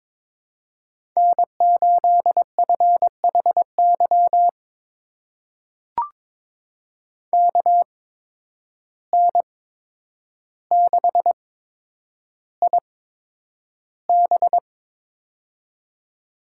Morse code, heard as a keyed tone, about 22 wpm, spaced as for 9 wpm, 700 Hz. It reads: N8FHY KN6IB
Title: USA Call Signs - 8x Character Spacing - 22wpm